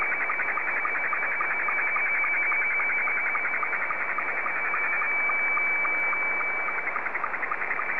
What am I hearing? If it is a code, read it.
SSTV (slow-scan television), Scottie 2.
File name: sstv / S2-ZL2TJM